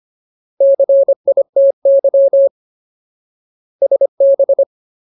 Morse code, heard as a keyed tone, about 25 wpm, 550 Hz. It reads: CITY SB
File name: Ultimate Word List / 25WPM Ultimate